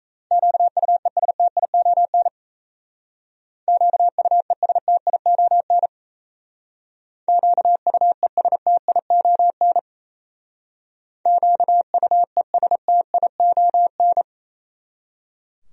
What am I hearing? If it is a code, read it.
Morse code, heard as a keyed tone, about 34 wpm, 700 Hz. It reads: QUESTION QUESTION QUESTION QUESTION
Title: Speed-Racing Top 500 Words 28wpm